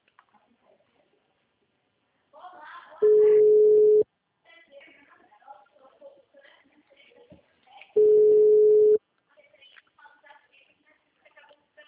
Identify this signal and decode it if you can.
Ringback tone (Europe, 425 Hz), with speech.